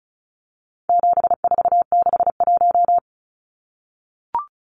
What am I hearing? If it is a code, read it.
Morse code, heard as a keyed tone, about 34 wpm, 700 Hz. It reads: 7461
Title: Numbers - 4-Digit 35wpm